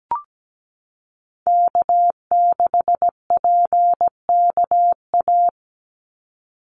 Morse code, heard as a keyed tone, about 17 wpm, 700 Hz. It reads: K6PKA
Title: USA Call Signs 17wpm